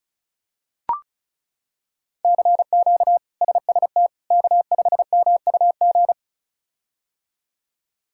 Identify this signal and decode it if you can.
Morse code, heard as a keyed tone, about 34 wpm, 700 Hz. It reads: CQ SST K5MUG
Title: K1USN SST QSO Practice - 35wpm